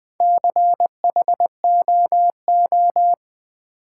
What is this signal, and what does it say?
Morse code, heard as a keyed tone, about 20 wpm, 700 Hz. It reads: CHOO
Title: Build A Word - 2 Letters At A Time - 6-Letter Words - 20wpm